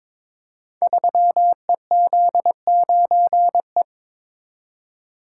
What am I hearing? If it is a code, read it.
Morse code, heard as a keyed tone, about 22 wpm, 700 Hz. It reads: 3EZ9E
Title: Code Groups - 5 Letter:Numbers - 22wpm